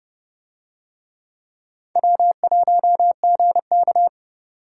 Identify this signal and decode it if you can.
Morse code, heard as a keyed tone, about 28 wpm, 700 Hz. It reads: W1GK